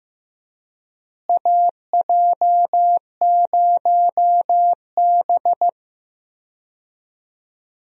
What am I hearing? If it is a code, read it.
Morse code, heard as a keyed tone, about 15 wpm, 700 Hz. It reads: AJ0B
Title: USA Call Signs - 4 Characters - Repeated 3x - 15wpm